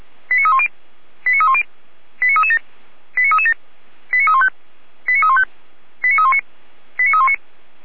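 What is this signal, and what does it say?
Five-tone selective call (ZVEI), heard as quick runs of five tones, tones of about 70 ms. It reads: 89310 89310 89337 89337 89315 89315 89319 89319
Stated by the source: ZVEI five-tone selective call received by SDR.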